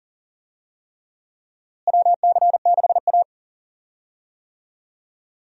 Morse code, heard as a keyed tone, about 40 wpm, 700 Hz. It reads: WC6A
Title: USA Call Signs - 4 Characters - Repeated 3x - 40wpm